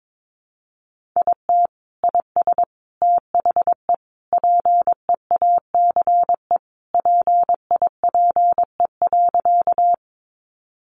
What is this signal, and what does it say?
Morse code, heard as a keyed tone, about 22 wpm, 700 Hz. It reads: IT IS THE PEACE PIPE.
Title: Sentences from Top 1500 Words 22wpm